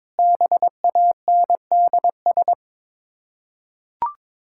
Morse code, heard as a keyed tone, about 21 wpm, 690 Hz. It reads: BANDS
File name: Difficult Words with Hints - 22wpm